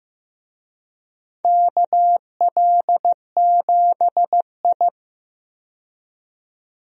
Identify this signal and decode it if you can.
Morse code, heard as a keyed tone, about 15 wpm, 700 Hz. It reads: KL7I
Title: DXCC-100 Call Signs - 4 Characters - 15wpm